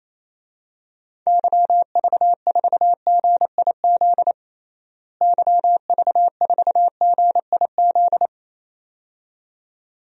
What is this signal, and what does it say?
Morse code, heard as a keyed tone, about 28 wpm, 700 Hz. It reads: YV4GIZ YV4GIZ
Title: DXCC-100 Call Signs - 6 Characters - Repeated 2x - 28wpm